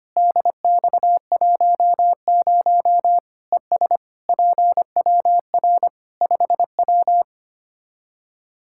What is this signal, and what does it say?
Morse code, heard as a keyed tone, about 25 wpm, 700 Hz. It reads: DX10 ES PWR 5W
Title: Practice QSO - Exchange 2 - Info - 25wpm